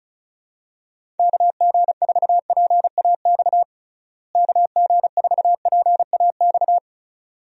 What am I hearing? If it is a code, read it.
Morse code, heard as a keyed tone, about 36 wpm, 700 Hz. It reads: KG4PAX KG4PAX
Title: USA Call Signs - 6 Characters - Repeated 2x - 35wpm